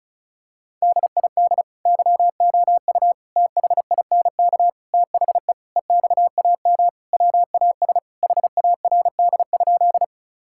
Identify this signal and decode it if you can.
Morse code, heard as a keyed tone, about 36 wpm, 700 Hz. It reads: DID YOU THINK THE EXAM WAS HARD?
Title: Sentences from Top 1000 Words - Review Part 5 - 35wpm